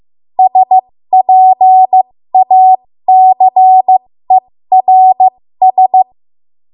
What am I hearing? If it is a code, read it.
Morse code, heard as a keyed tone, about 15 wpm, spaced as for 13 wpm, 750 Hz. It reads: SPACERS
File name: Morse / 13WPM